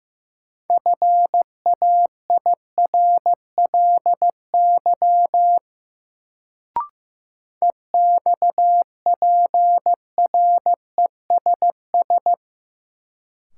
Morse code, heard as a keyed tone, about 15 wpm, 700 Hz. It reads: FAIRLY EXPRESS
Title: Sentences from Top 1100 Words - New Word Drill - 15wpm